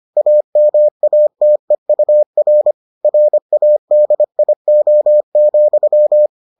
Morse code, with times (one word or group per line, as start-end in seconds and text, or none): 0.2-2.7 AMATEUR
3.0-6.3 RADIO,